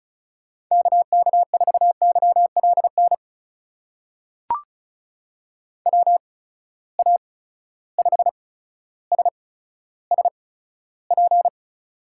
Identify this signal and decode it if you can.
Morse code, heard as a keyed tone, about 34 wpm, spaced as for 15 wpm, 700 Hz. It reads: KK4YLN WA5SSP
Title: USA Call Signs - 6 Characters - 8x Character Spacing - 35wpm